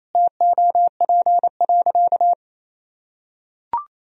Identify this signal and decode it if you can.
Morse code, heard as a keyed tone, about 28 wpm, 700 Hz. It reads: TOP.
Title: Sentences from Top 1000 Words - Review Part 2 - 28wpm